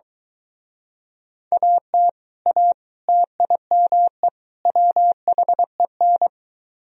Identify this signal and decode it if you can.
Morse code, heard as a keyed tone, about 23 wpm, 700 Hz. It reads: AT A TIME WHEN